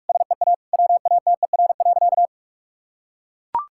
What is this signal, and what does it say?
Morse code, heard as a keyed tone, about 43 wpm, 700 Hz. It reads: SEA WATER.
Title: Sentences from Top 1000 Words - Review Part 5 - 45wpm